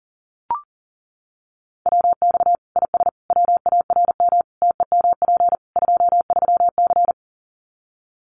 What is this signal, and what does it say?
Morse code, heard as a keyed tone, about 40 wpm, 700 Hz. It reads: WX IS WARM TEMP 23C
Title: Common QSO Phrases 40wpm